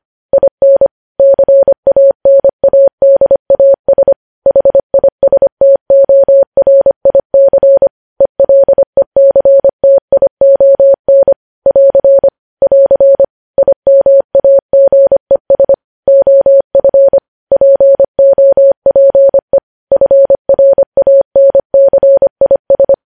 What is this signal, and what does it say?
Morse code, heard as a keyed tone, about 25 wpm, 550 Hz. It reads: IN CANADAS HISTORIC ELECTION <AR> <AR> IMAGES OF POPE FRANCIS
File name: podcast / News headlines-1x-25x25